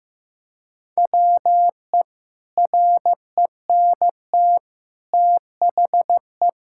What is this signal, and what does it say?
Morse code, heard as a keyed tone, about 15 wpm, 700 Hz. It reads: WE RENT THE
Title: Sentences from Top 1000 Words - Review Part 5 - 15wpm